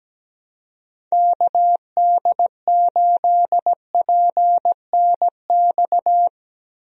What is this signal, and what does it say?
Morse code, heard as a keyed tone, about 17 wpm, 700 Hz. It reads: KD8PNX
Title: USA Call Signs - 6 Characters - Repeated 3x - 17wpm